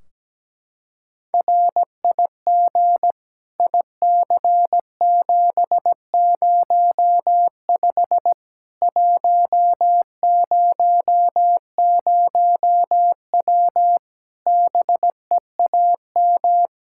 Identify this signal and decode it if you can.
Morse code, heard as a keyed tone, about 17 wpm, 700 Hz. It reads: RIG IC705 100W BEAM
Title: K1USN SST QSO Practice 17wpm